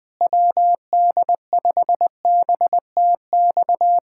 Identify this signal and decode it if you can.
Morse code, heard as a keyed tone, about 20 wpm, 700 Hz. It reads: WD5BTX
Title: Speed-Racing Common QSO Phrases Encore - 17wpm